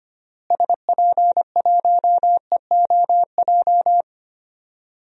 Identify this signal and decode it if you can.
Morse code, heard as a keyed tone, about 25 wpm, 700 Hz. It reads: SP1EOJ
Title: DXCC-100 Call Signs - 6 Characters - Repeated 3x - 25wpm